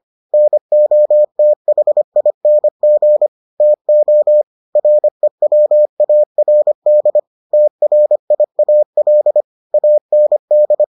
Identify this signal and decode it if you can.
Morse code, heard as a keyed tone, about 25 wpm, 600 Hz. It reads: NOTHING TO REWARD TRIAL AND